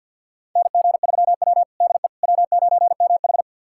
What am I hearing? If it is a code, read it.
Morse code, heard as a keyed tone, about 49 wpm, 700 Hz. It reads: NG3W DE W9MH
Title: USA Callsign DE Callsign - 4 Characters - 50wpm